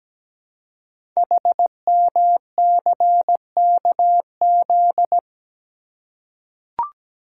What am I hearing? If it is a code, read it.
Morse code, heard as a keyed tone, about 17 wpm, 700 Hz. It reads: HMCKZ
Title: Code Groups - 5 Letters - 17wpm